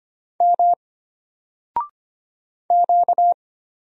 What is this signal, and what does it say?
Morse code, heard as a keyed tone, about 25 wpm, 700 Hz. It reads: M Q
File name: Single-Letter-Number-25wpm